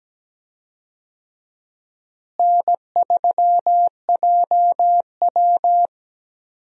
Morse code, heard as a keyed tone, about 17 wpm, 700 Hz. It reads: N3JW
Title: USA Call Signs - 4 Characters - Repeated 3x - 17wpm